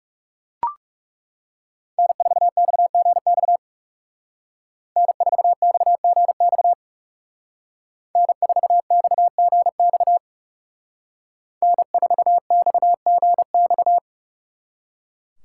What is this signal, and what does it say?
Morse code, heard as a keyed tone, about 36 wpm, 700 Hz. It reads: N4XGX N4XGX N4XGX N4XGX
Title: Speed Racing - USA 1x3 Call Signs 30wpm